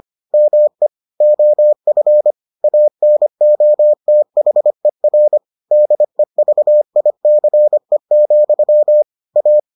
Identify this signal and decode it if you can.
Morse code, heard as a keyed tone, about 25 wpm, 600 Hz. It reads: ME OF ANOTHER DEVICE, A